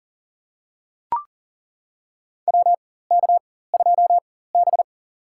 Morse code, heard as a keyed tone, about 40 wpm, spaced as for 19 wpm, 700 Hz. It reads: WK2B
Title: USA Call Signs - 4x Character Spacing - 40wpm